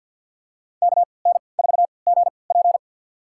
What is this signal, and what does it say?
Morse code, heard as a keyed tone, about 51 wpm, spaced as for 28 wpm, 690 Hz. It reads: KN4GP
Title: USA Call Signs - 5 Characters - 3x Character Spacing - 50wpm